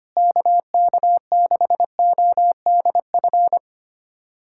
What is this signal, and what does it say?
Morse code, heard as a keyed tone, about 25 wpm, 700 Hz. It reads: KK6ODF
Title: USA Callsign DE Callsign 25wpm